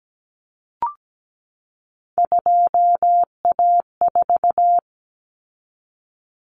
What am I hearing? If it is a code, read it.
Morse code, heard as a keyed tone, about 17 wpm, 700 Hz. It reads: 2A4
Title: Common Cut Numbers - 3-Digit 17wpm